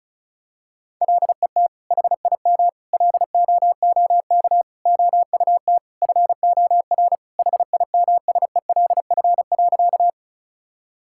Morse code, heard as a keyed tone, about 36 wpm, 700 Hz. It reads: LET HIM LOOK OUT FOR HIMSELF.
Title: Sentences from Top 1600 Words 35wpm